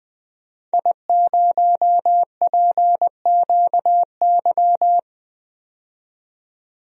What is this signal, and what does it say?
Morse code, heard as a keyed tone, about 20 wpm, 700 Hz. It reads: I0PQY
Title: DXCC-100 Call Signs - 5 Characters - 20wpm